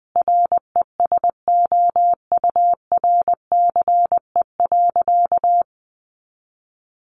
Morse code, heard as a keyed tone, about 19 wpm, 700 Hz. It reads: RESOURCE.
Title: Sentences from Top 1300 Words 20wpm